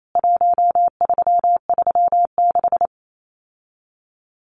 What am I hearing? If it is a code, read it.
Morse code, heard as a keyed tone, about 28 wpm, 700 Hz. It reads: 1336
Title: CWT Exchange 28wpm